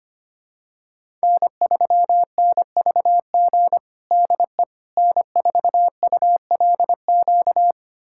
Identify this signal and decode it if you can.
Morse code, heard as a keyed tone, about 25 wpm, 700 Hz. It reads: N3NVG DE N4ULQ
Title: USA Callsign DE Callsign - 5 Characters - 25wpm